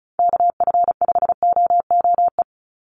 Morse code, heard as a keyed tone, about 34 wpm, 700 Hz. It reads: KF5OOE
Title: USA Call Signs 35wpm